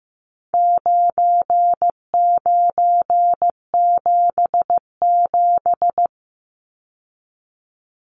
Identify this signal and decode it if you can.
Morse code, heard as a keyed tone, about 15 wpm, 700 Hz. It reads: 9977
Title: Numbers - 4-Digit 15wpm